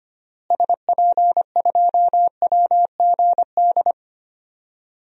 Morse code, heard as a keyed tone, about 25 wpm, 700 Hz. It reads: SP2WGD
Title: DXCC-100 Call Signs - 6 Characters - Repeated 2x - 25wpm